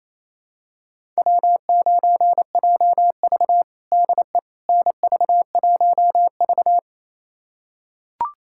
Morse code, heard as a keyed tone, about 28 wpm, 700 Hz. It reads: W9JV DE NV1V
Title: USA Callsign DE Callsign - 4 Characters - 28wpm